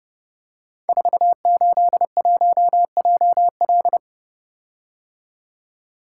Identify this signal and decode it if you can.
Morse code, heard as a keyed tone, about 31 wpm, 700 Hz. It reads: 481JL
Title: Code Groups - 5 Letter:Numbers - 30wpm